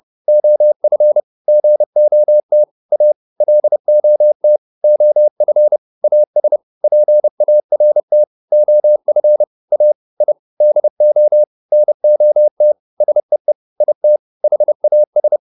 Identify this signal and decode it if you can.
Morse code, heard as a keyed tone, about 30 wpm, 600 Hz. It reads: OF GOT A LOT OF AS PART OF A I DO NOT SEE IT HAS